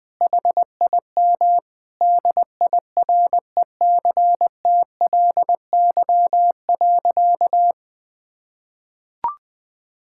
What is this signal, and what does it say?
Morse code, heard as a keyed tone, about 21 wpm, 700 Hz. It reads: HIM DIRECTLY.
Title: Sentences from Top 1000 Words - Review Part 5 - 20wpm